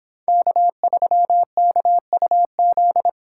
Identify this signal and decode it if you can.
Morse code, heard as a keyed tone, about 25 wpm, 700 Hz. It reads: K3KUZ
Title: Speed Racing - USA 1x3 Call Signs 22wpm